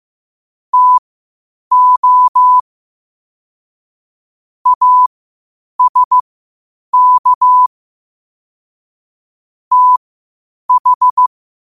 Morse code, heard as a keyed tone, about 15 wpm, spaced as for 8 wpm, 1000 Hz. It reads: TO ASK TH
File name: qotdfiles / 8wpm